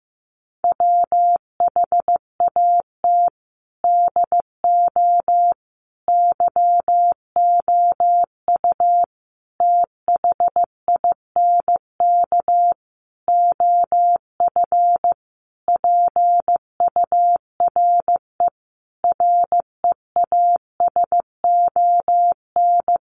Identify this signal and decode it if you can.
Morse code, heard as a keyed tone, about 15 wpm, 700 Hz. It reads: WHAT DO YOU THINK OF PURE REASON
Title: Sentences from Top 1500 Words 15wpm